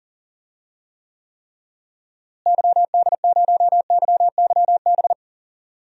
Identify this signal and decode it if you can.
Morse code, heard as a keyed tone, about 39 wpm, 700 Hz. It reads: YD0YYB
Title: DXCC-100 Call Signs - 6 Characters - Repeated 3x - 40wpm